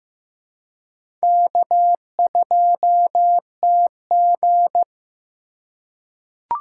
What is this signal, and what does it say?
Morse code, heard as a keyed tone, about 15 wpm, 700 Hz. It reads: K2TG
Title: USA Call Signs - 4 Characters - 4x Character Spacing - 15wpm